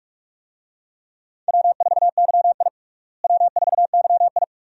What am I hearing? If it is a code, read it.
Morse code, heard as a keyed tone, about 42 wpm, 700 Hz. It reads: W4YI W4YI
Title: USA Call Signs - 4 Characters - Repeated 2x - 45wpm